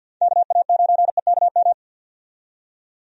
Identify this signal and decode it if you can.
Morse code, heard as a keyed tone, about 52 wpm, 700 Hz. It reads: KA9EKM
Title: USA Call Signs - 6 Characters - 50wpm